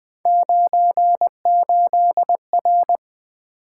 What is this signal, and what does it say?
Morse code, heard as a keyed tone, about 20 wpm, 700 Hz. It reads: 98R
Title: Three Letter-Number Combinations - 20wpm